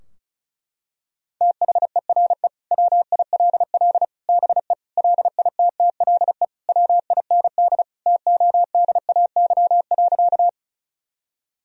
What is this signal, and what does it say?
Morse code, heard as a keyed tone, about 34 wpm, 700 Hz. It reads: THERE WILL BE LITTLE WIND TODAY.
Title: Sentences from Top 1000 Words - Review Part 2 - 35wpm